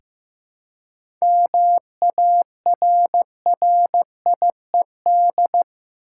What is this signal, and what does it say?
Morse code, heard as a keyed tone, about 15 wpm, 700 Hz. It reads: MARRIED
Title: Seven Letter Words - 15wpm